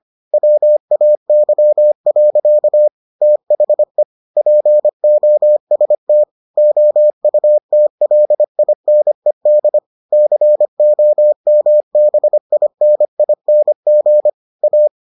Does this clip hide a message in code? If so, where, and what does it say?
Yes: Morse code from 0.3 to 14.9 s, heard as a keyed tone, about 25 wpm, 600 Hz. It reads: WAY. THE POST OUTLINED COMBINING A